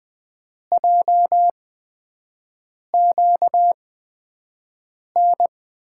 Morse code, heard as a keyed tone, about 20 wpm, spaced as for 5 wpm, 700 Hz. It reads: JQN
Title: USA Call Signs - 6 Characters - 8x Character Spacing - 20wpm